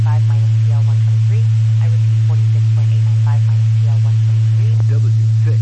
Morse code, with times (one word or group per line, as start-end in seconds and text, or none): none